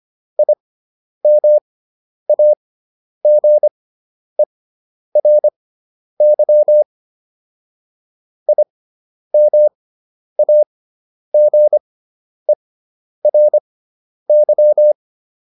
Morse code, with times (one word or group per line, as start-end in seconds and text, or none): 0.4-6.8 IMAGERY
8.5-14.9 IMAGERY